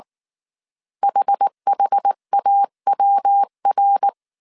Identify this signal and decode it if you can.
Morse code, heard as a keyed tone, about 19 wpm, 800 Hz. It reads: HHAWR
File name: CW / DET / 1 3 1500C S B 18wpm 1916